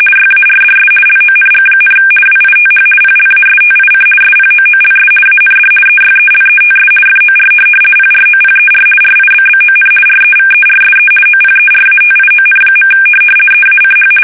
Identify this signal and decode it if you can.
RTTY, about 50 baud. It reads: YARDS LONG, WITH PLASTIC SPOOLS AND EYELETS FOR PROPER RIBBON REVERSE OPERATION. EACH
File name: fsk850-50B / teletype-corp